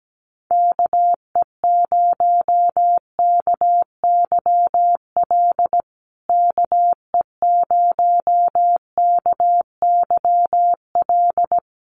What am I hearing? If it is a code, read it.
Morse code, heard as a keyed tone, about 17 wpm, 700 Hz. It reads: KE0KYL KE0KYL
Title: USA Call Signs Repeated 3x 17wpm